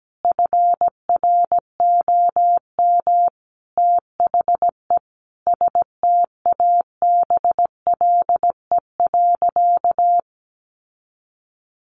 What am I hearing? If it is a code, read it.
Morse code, heard as a keyed tone, about 17 wpm, 700 Hz. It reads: FROM THE STABLE.
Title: Sentences from Top 1500 Words 17wpm